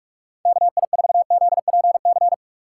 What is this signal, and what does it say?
Morse code, heard as a keyed tone, about 44 wpm, 700 Hz. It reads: KI4ZPC
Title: USA Call Signs - 6 Characters - 6x Character Spacing - 45wpm